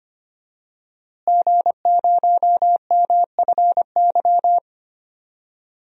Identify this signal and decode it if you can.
Morse code, heard as a keyed tone, about 25 wpm, 700 Hz. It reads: G0MFY
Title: DXCC-100 Call Signs - 5 Characters - Repeated 3x - 25wpm